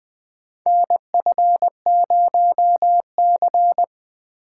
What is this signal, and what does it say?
Morse code, heard as a keyed tone, about 20 wpm, 700 Hz. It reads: NF0C
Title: USA Call Signs - 4 Characters - 3x Character Spacing - 20wpm